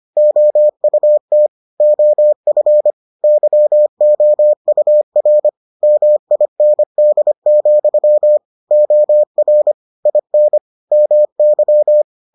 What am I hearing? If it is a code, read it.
Morse code, heard as a keyed tone, about 25 wpm, 600 Hz. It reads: OUT OF YOUR MIND, OR IN MY